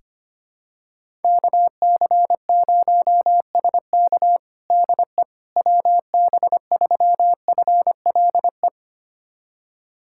Morse code, heard as a keyed tone, about 25 wpm, 700 Hz. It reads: KC0SK DE WB3FLE
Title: USA Callsign DE Callsign 25wpm